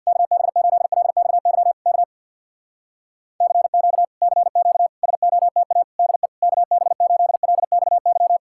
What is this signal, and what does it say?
Morse code, heard as a keyed tone, about 64 wpm, 700 Hz. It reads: KB8LXY K CQ CQ SOTA DE KB8LXY